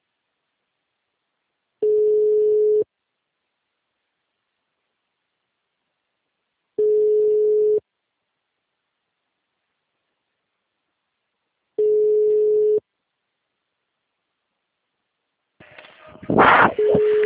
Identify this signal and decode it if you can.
Ringback tone (Europe, 425 Hz).
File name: out- -2038- - - .48771